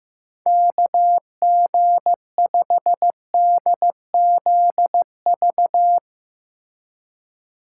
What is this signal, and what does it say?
Morse code, heard as a keyed tone, about 15 wpm, 700 Hz. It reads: KG5DZV
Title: USA Call Signs - 6 Characters - 2x Character Spacing - 15wpm